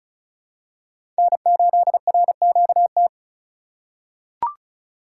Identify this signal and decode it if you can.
Morse code, heard as a keyed tone, about 36 wpm, 700 Hz. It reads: N8RQT
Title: USA Call Signs - 5 Characters - 6x Character Spacing - 35wpm